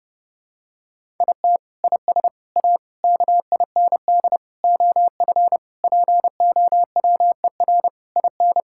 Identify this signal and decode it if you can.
Morse code, heard as a keyed tone, about 32 wpm, 700 Hz. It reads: IT IS A KIND OF POWER IN
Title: Sentences from Top 1000 Words - Review Part 5 - 30wpm